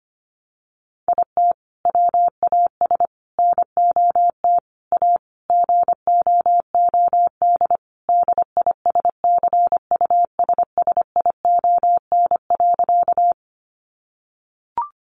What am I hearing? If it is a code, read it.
Morse code, heard as a keyed tone, about 25 wpm, 700 Hz. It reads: IT WAS NOT A GOOD DISCUSSION.
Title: Sentences from Top 1100 Words 25wpm